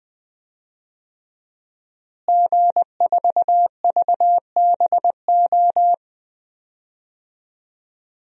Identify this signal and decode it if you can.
Morse code, heard as a keyed tone, about 20 wpm, 700 Hz. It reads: G4VBO
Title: DXCC-100 Call Signs - 5 Characters - Repeated 3x - 20wpm